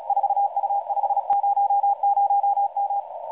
Morse code, heard as a keyed tone, about 34 wpm, 800 Hz, with radio noise. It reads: 4U500M
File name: 4U500M-CW 3507.1kHz